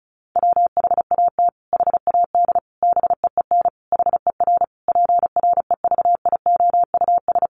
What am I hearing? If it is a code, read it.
Morse code, heard as a keyed tone, about 34 wpm, 700 Hz. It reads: WHAT HAD BEEN HER PREVIOUS
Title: Sentences from Top 1500 Words 35wpm